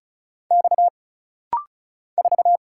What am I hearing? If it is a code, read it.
Morse code, heard as a keyed tone, about 34 wpm, 700 Hz. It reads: X 4
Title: Single Letter-Number Alt - Rapid-Fire 35wpm